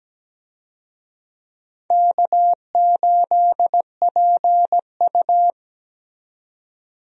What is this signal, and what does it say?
Morse code, heard as a keyed tone, about 17 wpm, 700 Hz. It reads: K8PU
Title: USA Call Signs - 4 Characters - 17wpm